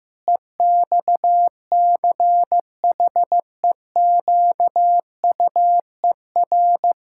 Morse code, heard as a keyed tone, about 15 wpm, 700 Hz. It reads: EXCHEQUER
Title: Pangram Challenge v2 - 15wpm